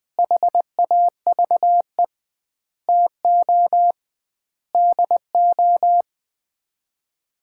Morse code, heard as a keyed tone, about 20 wpm, 700 Hz. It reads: HAVE TO DO
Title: 2x Word-Spacing - Sets of 5 Words - 20wpm